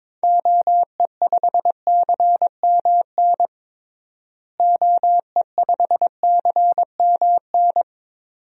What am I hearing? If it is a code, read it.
Morse code, heard as a keyed tone, about 22 wpm, 700 Hz. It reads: OE5CMN OE5CMN
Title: DXCC-100 Call Signs - 6 Characters - Repeated 2x - 22wpm